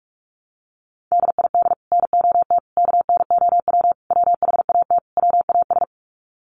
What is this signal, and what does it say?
Morse code, heard as a keyed tone, about 44 wpm, 700 Hz. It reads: DID NOT KNOW WHAT WAS
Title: Sets of 5 Words Encore2 45wpm